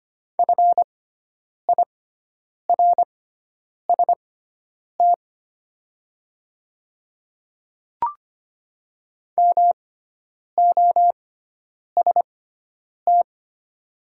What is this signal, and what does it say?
Morse code, heard as a keyed tone, about 25 wpm, spaced as for 9 wpm, 700 Hz. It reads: FIRST MOST
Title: 6x Character Spacing - Top 200 Words - 25wpm